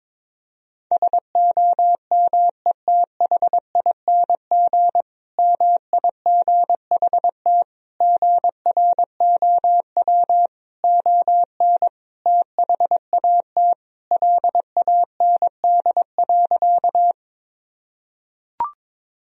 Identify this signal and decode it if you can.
Morse code, heard as a keyed tone, about 21 wpm, 700 Hz. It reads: SOMETHING MIGHT GROW ON THAT LAND.
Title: Sentences from Top 1000 Words - Review Part 1 - 22wpm